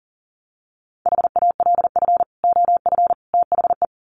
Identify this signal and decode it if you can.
Morse code, heard as a keyed tone, about 40 wpm, 700 Hz. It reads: HALF OF THE
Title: Sets of 3 Words 40wpm